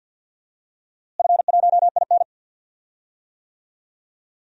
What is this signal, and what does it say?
Morse code, heard as a keyed tone, about 50 wpm, 700 Hz. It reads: F1IN
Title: DXCC-100 Call Signs - 4 Characters - Repeated 2x - 50wpm